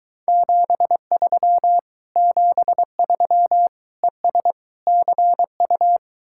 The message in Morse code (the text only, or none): 73 73 ES CU